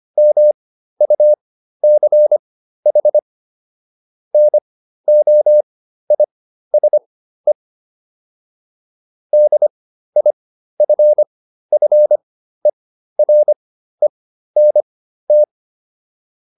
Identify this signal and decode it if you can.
Morse code, heard as a keyed tone, about 25 wpm, spaced as for 13 wpm, 600 Hz. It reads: MUCH NOISE DIFFERENT